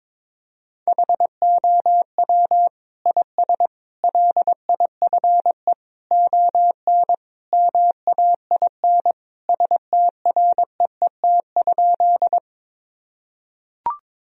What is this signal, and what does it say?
Morse code, heard as a keyed tone, about 22 wpm, 700 Hz. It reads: HOW IS LIFE ON MAIN STREET?
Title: Sentences from Top 1000 Words - Review Part 2 - 22wpm